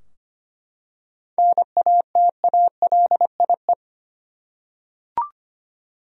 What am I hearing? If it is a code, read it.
Morse code, heard as a keyed tone, about 25 wpm, 700 Hz. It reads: NATALIE
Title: USA Common Names - 25wpm